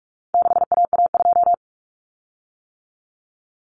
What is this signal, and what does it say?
Morse code, heard as a keyed tone, about 44 wpm, 700 Hz. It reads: 6AA2
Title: Common Cut Numbers - 4-Digit 45wpm